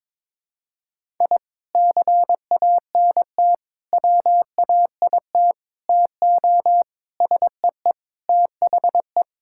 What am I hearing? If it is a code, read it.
Morse code, heard as a keyed tone, about 22 wpm, 700 Hz. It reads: I CANT WAIT TO SEE THE
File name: practice / 22wpm / Sentences with 3-Word Phrases - 22wpm